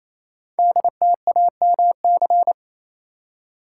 Morse code, heard as a keyed tone, about 27 wpm, 700 Hz. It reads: DTAMC
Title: Code Groups - 5 Letters - 28wpm